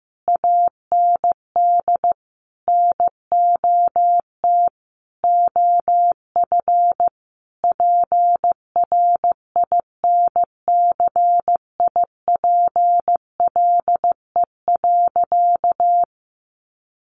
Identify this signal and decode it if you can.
Morse code, heard as a keyed tone, about 15 wpm, 700 Hz. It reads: AND NOT OF PRINCIPLE.
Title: Sentences from Top 1400 Words 15wpm